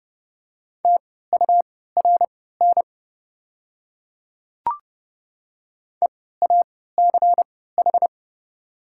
Morse code, heard as a keyed tone, about 30 wpm, spaced as for 17 wpm, 700 Hz. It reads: TURN EACH
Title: 3x Character Spacing - Top 200 Words - 30wpm